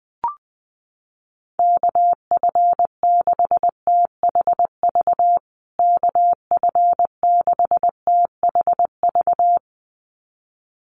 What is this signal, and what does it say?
Morse code, heard as a keyed tone, about 20 wpm, 700 Hz. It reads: KF6THV KF6THV
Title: USA Call Signs Repeated 2x 20wpm